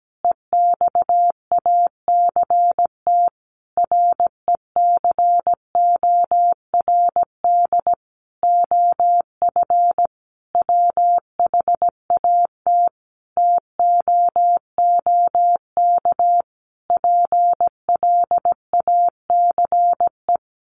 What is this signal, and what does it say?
Morse code, heard as a keyed tone, about 17 wpm, 700 Hz. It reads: EXACT RECORD OF WHAT TOOK PLACE
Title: Sentences from Top 1300 Words 17wpm